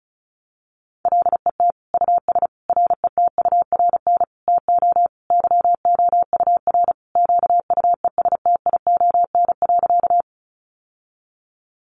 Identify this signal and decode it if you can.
Morse code, heard as a keyed tone, about 34 wpm, 700 Hz. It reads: LET US RETURN TO YOUR QUESTION.
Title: Sentences from Top 800 Words 35wpm